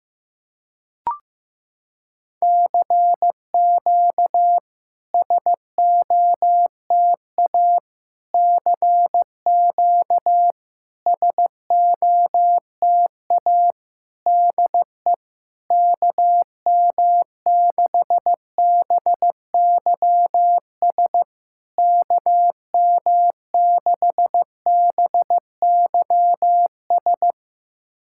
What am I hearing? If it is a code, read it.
Morse code, heard as a keyed tone, about 16 wpm, 690 Hz. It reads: CQ SOTA CQ SOTA DE KM6BYS KM6BYS
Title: SOTA QSO Practice 15wpm